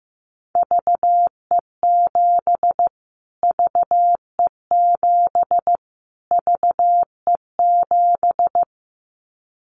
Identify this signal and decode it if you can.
Morse code, heard as a keyed tone, about 14 wpm, 700 Hz. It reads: VE7 VE7 VE7
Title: Callsign Prefixes for Top 25 DXCC - Repeated 3x - 15wpm